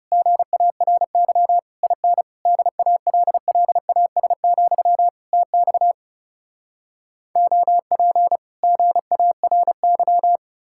Morse code, heard as a keyed tone, about 40 wpm, 700 Hz. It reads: GARY IN DALLAS, TX OP GARY